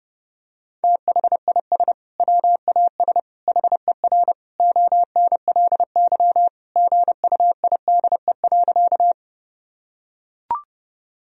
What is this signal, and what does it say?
Morse code, heard as a keyed tone, about 30 wpm, 700 Hz. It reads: THIS WAS HER ONLY GUIDE.
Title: Sentences from Top 1000 Words - Review Part 3 - 30wpm